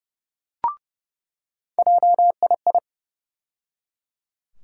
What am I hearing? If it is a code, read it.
Morse code, heard as a keyed tone, about 31 wpm, 700 Hz. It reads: JII
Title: USA Call Sign Postfixes - 3 Letters 30wpm